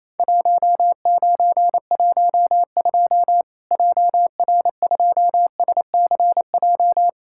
Morse code, heard as a keyed tone, about 28 wpm, 700 Hz. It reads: 1912 JR2SCJ
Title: Worked All Europe DX Contest - QTC Exchange - 28wpm